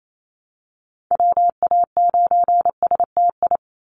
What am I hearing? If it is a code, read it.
Morse code, heard as a keyed tone, about 28 wpm, 700 Hz. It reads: WA9STI
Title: NA SOTA Activator Callsigns 28wpm